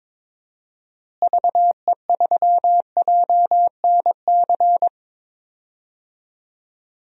Morse code, heard as a keyed tone, about 22 wpm, 700 Hz. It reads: VE3JNC
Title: DXCC-100 Call Signs - 6 Characters - Repeated 2x - 22wpm